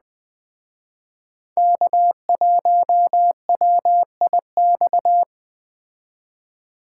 Morse code, heard as a keyed tone, about 20 wpm, 700 Hz. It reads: K1WIX